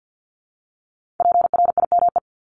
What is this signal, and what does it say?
Morse code, heard as a keyed tone, about 50 wpm, 700 Hz. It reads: PRIME
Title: Multi-letter Phonemes 50wpm